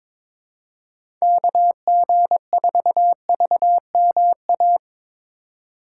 Morse code, heard as a keyed tone, about 22 wpm, 700 Hz. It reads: KG4VMA
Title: USA Call Signs - 6 Characters - 2x Character Spacing - 22wpm